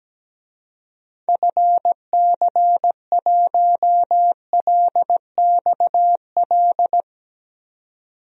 Morse code, heard as a keyed tone, about 17 wpm, 700 Hz. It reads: FC1LXL
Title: DXCC-100 Call Signs - 6 Characters - 17wpm